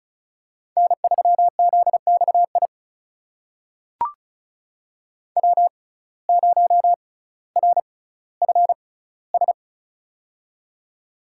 Morse code, heard as a keyed tone, about 34 wpm, spaced as for 18 wpm, 700 Hz. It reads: N3ZXI W0RFS
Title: USA Call Signs - 5 Characters - 6x Character Spacing - 35wpm